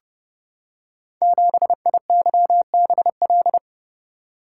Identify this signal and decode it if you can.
Morse code, heard as a keyed tone, about 31 wpm, 700 Hz. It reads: 7IYBL
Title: Code Groups - 5 Letter:Numbers - 30wpm